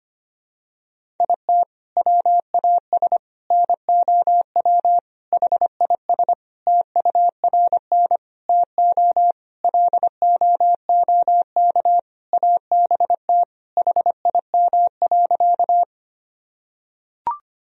Morse code, heard as a keyed tone, about 25 wpm, 700 Hz. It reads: IT WAS NOW HIS TURN TO LOOK ABT HIM.
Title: Sentences from Top 200 Words and Abbreviations 25wpm